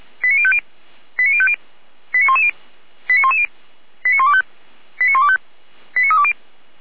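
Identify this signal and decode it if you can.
Five-tone selective call (ZVEI), heard as quick runs of five tones, tones of about 70 ms. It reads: 89050 89050 89110 89110 89125 89125 89320
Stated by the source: ZVEI five-tone call received by SDR.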